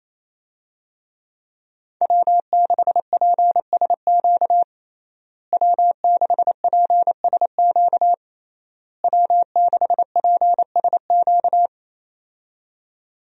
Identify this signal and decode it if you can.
Morse code, heard as a keyed tone, about 28 wpm, 700 Hz. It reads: W6PSQ W6PSQ W6PSQ
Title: USA Call Signs - 5 Characters - Repeated 3x - 28wpm